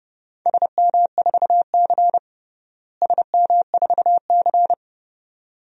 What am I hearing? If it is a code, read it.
Morse code, heard as a keyed tone, about 30 wpm, 700 Hz. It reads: SM4C SM4C
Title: DXCC-100 Call Signs - 4 Characters - Repeated 2x - 30wpm